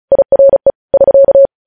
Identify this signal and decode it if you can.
Morse code, heard as a keyed tone, about 36 wpm, 550 Hz. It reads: IRE <SK>
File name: podcast / News headlines-1x-35x35